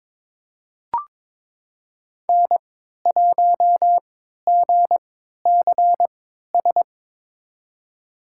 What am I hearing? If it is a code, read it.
Morse code, heard as a keyed tone, about 22 wpm, spaced as for 12 wpm, 700 Hz. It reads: N1GCS
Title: USA Call Signs - 5 Characters - 3x Character Spacing - 22wpm